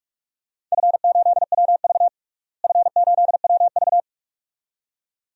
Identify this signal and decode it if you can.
Morse code, heard as a keyed tone, about 46 wpm, 700 Hz. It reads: F8WV F8WV
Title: DXCC-100 Call Signs - 4 Characters - Repeated 2x - 45wpm